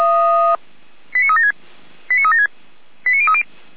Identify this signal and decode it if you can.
Five-tone selective call (ZVEI), heard as quick runs of five tones, tones of about 70 ms. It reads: 89376 89376 89030
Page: /455/SDR/ZVEI/89115